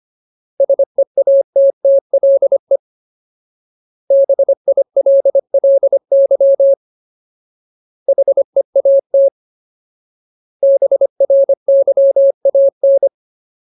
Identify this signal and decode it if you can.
Morse code, heard as a keyed tone, about 25 wpm, 550 Hz. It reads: SEATTLE BILLY HEAT BRYAN
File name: Ultimate Word List / 25WPM Ultimate